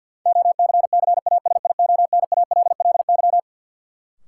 Morse code, heard as a keyed tone, about 53 wpm, 700 Hz. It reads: OCCASIONALLY
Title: Sentences from Top 1300 Words - New Word Drill Alt - 50wpm